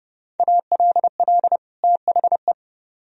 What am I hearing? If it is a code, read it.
Morse code, heard as a keyed tone, about 38 wpm, 700 Hz. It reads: ALL THE